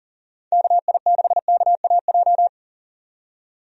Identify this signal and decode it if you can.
Morse code, heard as a keyed tone, about 40 wpm, 700 Hz. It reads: KI6KAJ
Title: USA Call Signs - 6 Characters - 40wpm